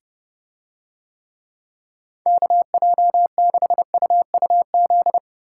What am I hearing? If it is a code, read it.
Morse code, heard as a keyed tone, about 29 wpm, 700 Hz. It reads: KJ6UUZ